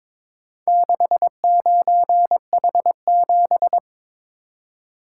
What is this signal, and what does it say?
Morse code, heard as a keyed tone, about 22 wpm, 700 Hz. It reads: 69H7
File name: practice / 22wpm / Four Letter-Number Combinations - 22wpm